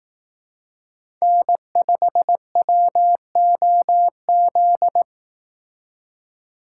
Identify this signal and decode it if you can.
Morse code, heard as a keyed tone, about 18 wpm, 700 Hz. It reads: N5WOZ
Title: Speed Racing - USA 1x3 Call Signs 15wpm